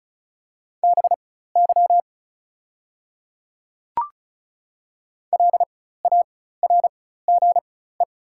Morse code, heard as a keyed tone, about 34 wpm, spaced as for 16 wpm, 700 Hz. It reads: BY LARGE
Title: 4x Character Spacing - Top 200 Words - 35wpm